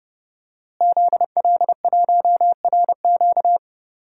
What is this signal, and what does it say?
Morse code, heard as a keyed tone, about 30 wpm, 700 Hz. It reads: ZL1RQ
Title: DXCC-100 Call Signs - 5 Characters - Repeated 2x - 30wpm